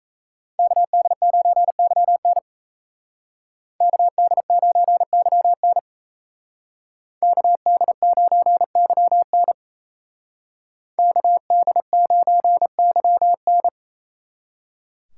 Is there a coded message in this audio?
Yes: Morse code from 0.6 to 13.7 s, heard as a keyed tone, about 36 wpm, 700 Hz. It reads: KD9YN KD9YN KD9YN KD9YN